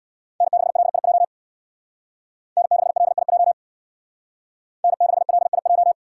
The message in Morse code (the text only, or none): N6LIJ N6LIJ N6LIJ